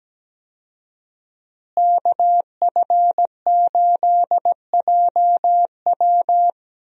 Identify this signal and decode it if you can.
Morse code, heard as a keyed tone, about 17 wpm, 700 Hz. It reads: KF8JW
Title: USA Call Signs - 5 Characters - Repeated 2x - 17wpm